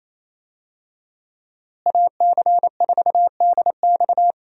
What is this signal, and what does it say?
Morse code, heard as a keyed tone, about 27 wpm, 700 Hz. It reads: AC4DX
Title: USA Call Signs - 5 Characters - Repeated 3x - 28wpm